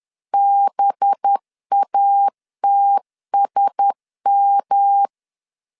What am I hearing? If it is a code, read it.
Morse code, heard as a keyed tone, about 10 wpm, 800 Hz. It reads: BATSM
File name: LEZIONI / 8 YJ45 / 8 Y 10wpm 2615